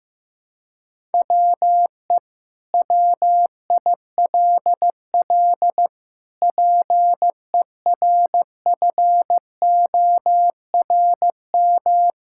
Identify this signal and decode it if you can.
Morse code, heard as a keyed tone, about 16 wpm, 700 Hz. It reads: WE WILL PERFORM
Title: Sentences from Top 1000 Words - Review Part 5 - 15wpm